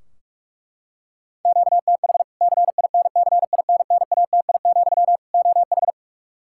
Morse code, heard as a keyed tone, about 41 wpm, 700 Hz. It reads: QTH CINCINNATI, OH
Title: Practice QSO - Exchange 1 - Beginning - 45wpm